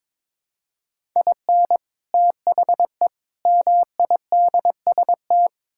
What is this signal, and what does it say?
Morse code, heard as a keyed tone, about 22 wpm, 700 Hz. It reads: IN THE MIDST
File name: practice / 22wpm / Build A Phrase - 5-Word Phrases - 22wpm